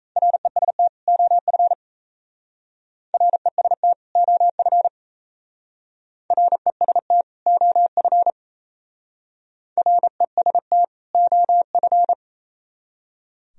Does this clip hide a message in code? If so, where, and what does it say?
Yes: Morse code from 0.2 to 12.1 s, heard as a keyed tone, about 35 wpm, 700 Hz. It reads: REST OF REST OF REST OF REST OF